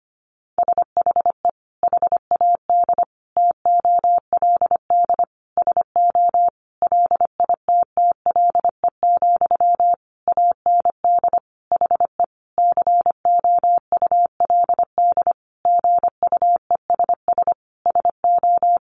Morse code, heard as a keyed tone, about 25 wpm, 700 Hz. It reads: SHE HAD TOLD SO LITTLE, AND HE COULD GUESS SO
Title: Sentences from Top 1500 Words 25wpm